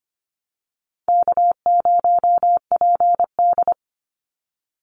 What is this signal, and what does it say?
Morse code, heard as a keyed tone, about 25 wpm, 700 Hz. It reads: K0PD
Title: Easy Call Signs Repeated 2x - 25wpm